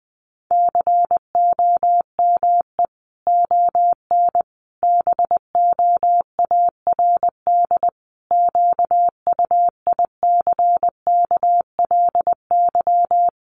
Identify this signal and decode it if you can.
Morse code, heard as a keyed tone, about 20 wpm, 700 Hz. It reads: COME ON BOARD QUICKLY
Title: Sentences from Top 1300 Words 20wpm